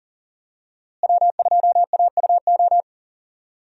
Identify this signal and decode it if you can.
Morse code, heard as a keyed tone, about 40 wpm, 700 Hz. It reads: W2AUO
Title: USA Call Signs - 5 Characters - Repeated 3x - 40wpm